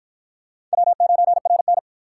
Morse code, heard as a keyed tone, about 53 wpm, 700 Hz. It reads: W9RN